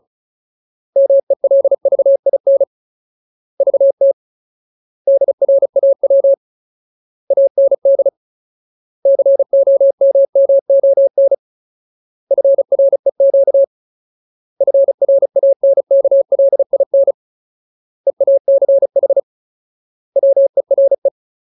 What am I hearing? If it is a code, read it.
Morse code, heard as a keyed tone, about 34 wpm, 550 Hz. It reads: MELVIN VT DRAW AND COMMON FREQ FRANKLIN EACH WERE
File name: Ultimate Word List / 35WPM Ultimate